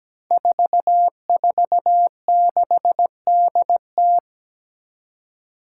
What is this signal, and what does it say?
Morse code, heard as a keyed tone, about 17 wpm, 700 Hz. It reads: 446DT
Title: Five Challenging Letter-Number Combinations 17wpm